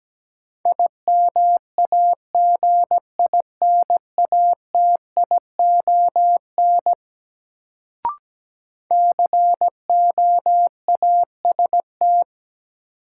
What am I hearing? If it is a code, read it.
Morse code, heard as a keyed tone, about 17 wpm, 700 Hz. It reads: IMAGINATION COAST
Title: Sentences from Top 1300 Words - New Word Drill Alt - 17wpm